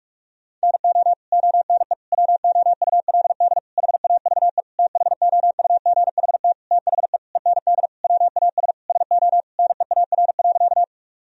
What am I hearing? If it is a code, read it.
Morse code, heard as a keyed tone, about 47 wpm, 700 Hz. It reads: NO ONE WOULD HAVE THOUGHT THE END WAS SO NEAR.
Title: Sentences from Top 1000 Words - Review Part 1 - 45wpm